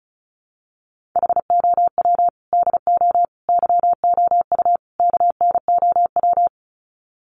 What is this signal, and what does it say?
Morse code, heard as a keyed tone, about 34 wpm, 700 Hz. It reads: HOW DO YOU KNOW
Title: Sets of 4 Words 35wpm